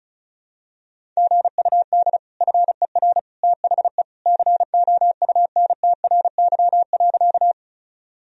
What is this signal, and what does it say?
Morse code, heard as a keyed tone, about 36 wpm, 700 Hz. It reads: GUD FER THE COUNTRY.
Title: Sentences from Top 300 Words and Abbreviations 35wpm